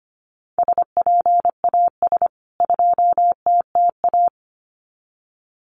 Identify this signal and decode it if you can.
Morse code, heard as a keyed tone, about 25 wpm, 700 Hz. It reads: SPAS 2TTA
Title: CWT Exchange 25wpm